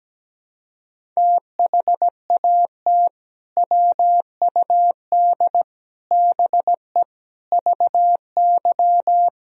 Morse code, heard as a keyed tone, about 17 wpm, 700 Hz. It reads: THAT WUD BE VY
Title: Sentences from Top 200 Words and Abbreviations 17wpm